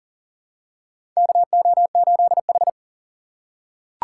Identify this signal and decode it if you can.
Morse code, heard as a keyed tone, about 40 wpm, 700 Hz. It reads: KO8H
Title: USA Call Signs - 4x Character Spacing - 40wpm